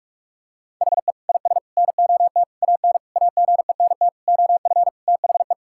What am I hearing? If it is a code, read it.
Morse code, heard as a keyed tone, about 42 wpm, 700 Hz. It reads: HE IS NOT AN AGENT OF THE
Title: Sentences from Top 1600 Words 45wpm